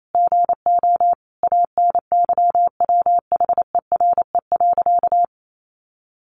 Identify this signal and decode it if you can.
Morse code, heard as a keyed tone, about 28 wpm, 700 Hz. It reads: GO ANYWHERE.
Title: Sentences from Top 1400 Words 28wpm